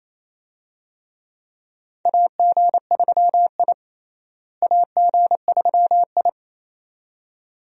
Morse code, heard as a keyed tone, about 27 wpm, 700 Hz. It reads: AG3I AG3I
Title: USA Call Signs - 4 Characters - Repeated 2x - 28wpm